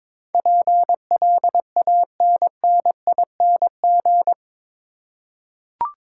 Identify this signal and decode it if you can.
Morse code, heard as a keyed tone, about 22 wpm, 700 Hz. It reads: PLANNING